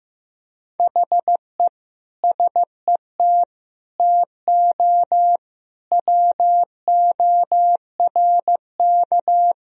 Morse code, heard as a keyed tone, about 16 wpm, 700 Hz. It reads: HE SET TO WORK
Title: Sentences from Top 1000 Words - Review Part 1 - 15wpm